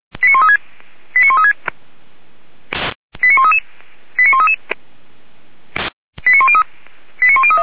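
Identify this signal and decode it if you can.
Five-tone selective call (ZVEI), heard as quick runs of five tones, tones of about 70 ms. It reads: 89136 89136 89133 89133 89103
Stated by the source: ZVEI five-tone call 89131 received on SDR.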